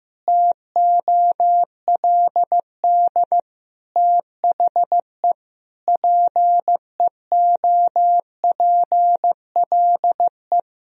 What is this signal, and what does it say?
Morse code, heard as a keyed tone, about 15 wpm, 700 Hz. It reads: TOLD THE PEOPLE
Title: Sentences from Top 1000 Words - Review Part 2 - 15wpm